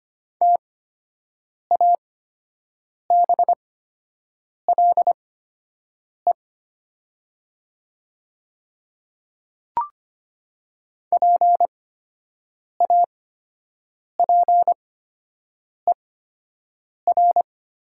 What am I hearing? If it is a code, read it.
Morse code, heard as a keyed tone, about 25 wpm, spaced as for 7 wpm, 690 Hz. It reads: TABLE PAPER
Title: 8x Character Spacing - Top 500 Words - 25wpm